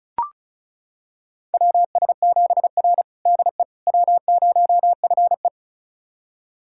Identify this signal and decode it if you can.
Morse code, heard as a keyed tone, about 34 wpm, 700 Hz. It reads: WS7R DE W0FE
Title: USA Callsign DE Callsign - 4 Characters - 35wpm